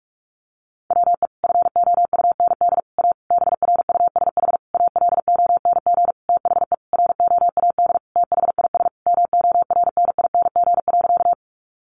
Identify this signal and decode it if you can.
Morse code, heard as a keyed tone, about 44 wpm, 700 Hz. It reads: WE FOUND A BRUSH ALONG THE ROAD THIS MORNING.
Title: Sentences from Top 1100 Words 45wpm